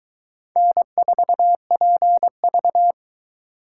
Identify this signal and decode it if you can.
Morse code, heard as a keyed tone, about 24 wpm, 700 Hz. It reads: N4PV